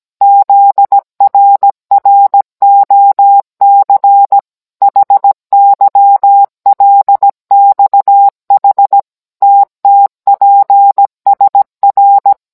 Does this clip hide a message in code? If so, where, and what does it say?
Yes: Morse code from 0.2 to 12.3 s, heard as a keyed tone, about 17 wpm, 800 Hz. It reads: ZRROC HYLXH TTPSR